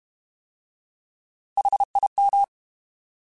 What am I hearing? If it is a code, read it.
Morse code, heard as a keyed tone, about 32 wpm, 800 Hz. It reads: HIM